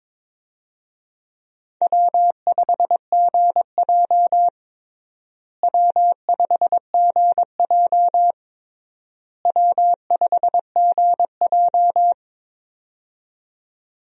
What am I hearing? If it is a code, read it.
Morse code, heard as a keyed tone, about 21 wpm, 700 Hz. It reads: W5GJ W5GJ W5GJ
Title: USA Call Signs - 4 Characters - Repeated 3x - 22wpm